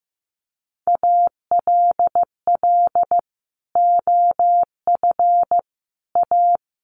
Morse code, heard as a keyed tone, about 15 wpm, 700 Hz. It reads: ALL OF A
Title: Sets of 4 Words 15wpm